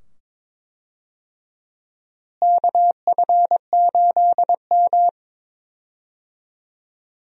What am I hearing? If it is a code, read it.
Morse code, heard as a keyed tone, about 22 wpm, 700 Hz. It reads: KF8M
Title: USA Call Signs - 4 Characters - Repeated 2x - 22wpm